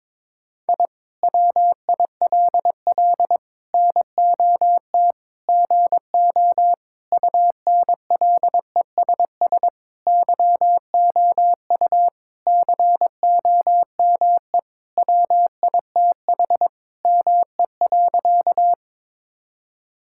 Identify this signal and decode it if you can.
Morse code, heard as a keyed tone, about 22 wpm, 700 Hz. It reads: I WILL NOT GO UNLESS YOU COME WITH ME.
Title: Sentences from Top 1000 Words - Review Part 6 - 22wpm